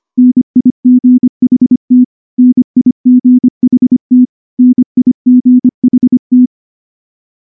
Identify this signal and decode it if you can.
Morse code, heard as a keyed tone, about 23 wpm, 250 Hz. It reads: NIGHT NIGHT NIGHT
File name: morse / morse-training-25WPM